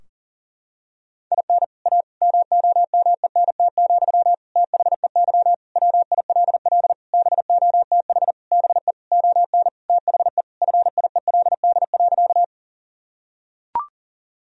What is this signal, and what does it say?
Morse code, heard as a keyed tone, about 40 wpm, 700 Hz. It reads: IN A MOMENT, THEY WILL BOTH BE ON THE FIELD.
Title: Sentences from Top 1000 Words - Review Part 2 - 40wpm